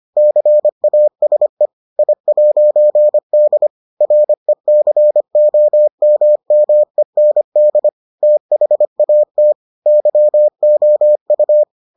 Morse code, heard as a keyed tone, about 25 wpm, 600 Hz. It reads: CASE I'D RECOMMEND THAT YOU